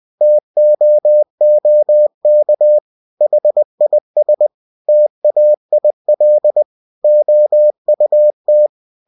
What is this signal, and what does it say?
Morse code, heard as a keyed tone, about 20 wpm, 600 Hz. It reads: TOOK HIS TAIL OUT
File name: cw-books / winnie-the-pooh / winnie-20wpm-page-0088